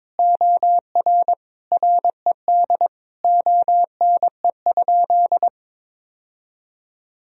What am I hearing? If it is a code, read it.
Morse code, heard as a keyed tone, about 22 wpm, 700 Hz. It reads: OR RED ONE?
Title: Sentences from Top 1000 Words - Review Part 2 - 22wpm